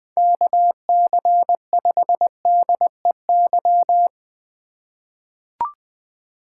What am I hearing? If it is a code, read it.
Morse code, heard as a keyed tone, about 21 wpm, 700 Hz. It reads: KC5DEY
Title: USA Call Signs - 2x Character Spacing - 20wpm